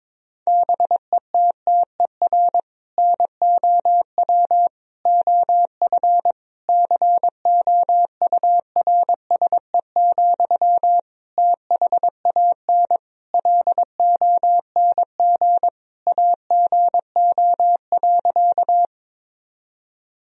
Morse code, heard as a keyed tone, about 22 wpm, 700 Hz. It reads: BETTER NOW OF COURSE, THAN LONG AGO.
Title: Sentences from Top 1000 Words - Review Part 3 - 22wpm